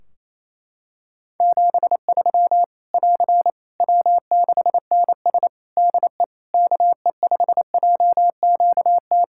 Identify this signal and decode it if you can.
Morse code, heard as a keyed tone, about 28 wpm, 700 Hz. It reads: 73 <AR> W6NS DE KE5JQT
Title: Practice QSO - Exchange 4 - Ending - 28wpm